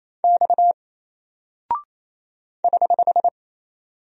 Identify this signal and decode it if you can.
Morse code, heard as a keyed tone, about 27 wpm, 700 Hz. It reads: X <HH>
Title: Single Character (Letter:Number:Punctuation:Prosign) Alt 28wpm